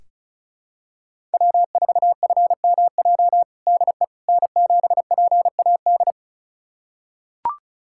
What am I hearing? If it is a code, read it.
Morse code, heard as a keyed tone, about 34 wpm, 700 Hz. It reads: W4FMJ DE N7PAD
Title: USA Callsign DE Callsign - 5 Characters - 35wpm